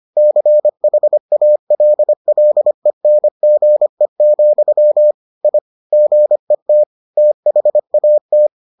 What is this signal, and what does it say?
Morse code, heard as a keyed tone, about 25 wpm, 600 Hz. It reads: CHALLENGE, I GET THAT